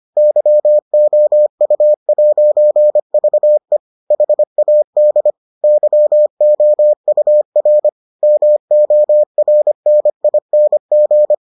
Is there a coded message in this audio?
Yes: Morse code from 0.2 to 11.3 s, heard as a keyed tone, about 25 wpm, 600 Hz. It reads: YOU'VE HAD YOUR MORNING